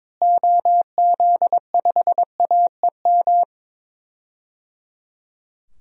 Morse code, heard as a keyed tone, about 22 wpm, 700 Hz. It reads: OZ5AEM
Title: DXCC-100 Call Signs - 6 Characters - Repeated 3x - 22wpm